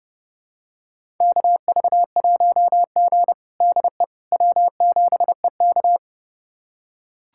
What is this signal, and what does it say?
Morse code, heard as a keyed tone, about 30 wpm, 700 Hz. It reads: KV1G DE W7EK
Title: USA Callsign DE Callsign - 4 Characters - 30wpm